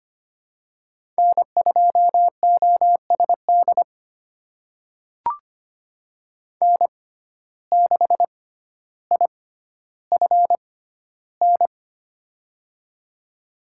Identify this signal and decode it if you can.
Morse code, heard as a keyed tone, about 25 wpm, spaced as for 13 wpm, 700 Hz. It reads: N2OSD N6IFN
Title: USA Call Signs - 6x Character Spacing - 25wpm